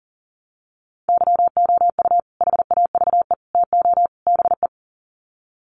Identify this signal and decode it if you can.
Morse code, heard as a keyed tone, about 40 wpm, 700 Hz. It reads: YOU HAVE TO BE
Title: Sets of 4 Words 40wpm